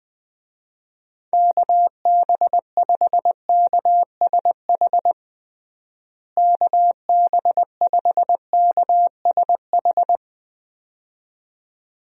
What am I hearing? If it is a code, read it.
Morse code, heard as a keyed tone, about 20 wpm, 700 Hz. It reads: KB5KSH KB5KSH
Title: USA Call Signs - 6 Characters - Repeated 2x - 20wpm